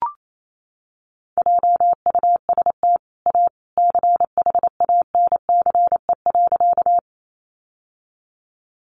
Morse code, heard as a keyed tone, about 28 wpm, 700 Hz. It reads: JUST A CHANCE.
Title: Sentences from Top 800 Words 28wpm